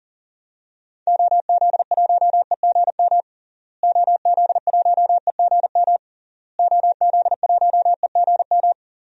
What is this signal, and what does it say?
Morse code, heard as a keyed tone, about 37 wpm, 700 Hz. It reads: OZ1EGM OZ1EGM OZ1EGM
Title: DXCC-100 Call Signs - 6 Characters - Repeated 3x - 40wpm